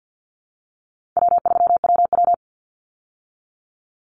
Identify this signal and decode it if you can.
Morse code, heard as a keyed tone, about 50 wpm, 700 Hz. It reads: W3WW
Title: Call Signs - 50wpm